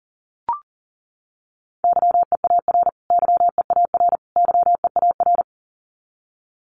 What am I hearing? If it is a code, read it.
Morse code, heard as a keyed tone, about 40 wpm, 700 Hz. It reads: YEAR YEAR YEAR
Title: Top 200 Words Repeated 40wpm